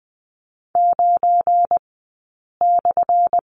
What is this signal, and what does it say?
Morse code, heard as a keyed tone, about 20 wpm, spaced as for 8 wpm, 700 Hz. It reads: 9/
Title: 177 Lesson 29d - Slash - Two Characters 20wpm